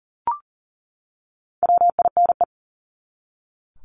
Morse code, heard as a keyed tone, about 40 wpm, 700 Hz. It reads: WINE
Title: Top 1501-1750 Words 40wpm